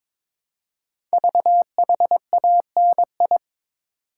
Morse code, heard as a keyed tone, about 22 wpm, 700 Hz. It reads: VHANI
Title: Code Groups - 5 Letters - 22wpm